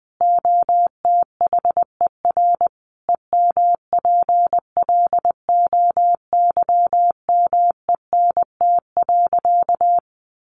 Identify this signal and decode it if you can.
Morse code, heard as a keyed tone, about 20 wpm, 700 Hz. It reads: OTHER EMPLOYMENT.
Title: Sentences from Top 1500 Words 20wpm